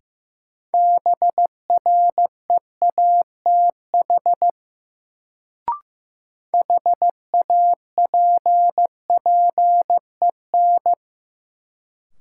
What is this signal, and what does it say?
Morse code, heard as a keyed tone, about 15 wpm, 700 Hz. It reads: BREATH HAPPEN
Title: Sets of 5 Words Encore - New Word Drill - 15wpm